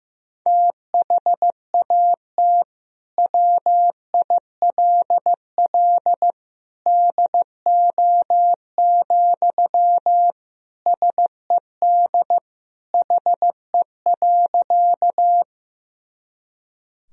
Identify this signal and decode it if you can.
Morse code, heard as a keyed tone, about 15 wpm, 700 Hz. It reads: THAT WILL DO, SED HE.
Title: Sentences from Top 100 Words and Abbreviations 15wpm